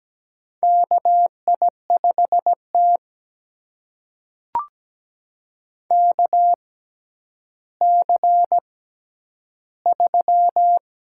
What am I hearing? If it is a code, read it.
Morse code, heard as a keyed tone, about 17 wpm, spaced as for 10 wpm, 700 Hz. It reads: KI5T KC3
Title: USA Call Signs - 4 Characters - 6x Character Spacing - 17wpm